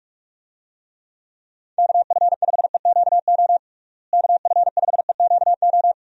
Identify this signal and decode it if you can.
Morse code, heard as a keyed tone, about 46 wpm, 700 Hz. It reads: KF5EQO KF5EQO
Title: USA Call Signs - 6 Characters - Repeated 2x - 45wpm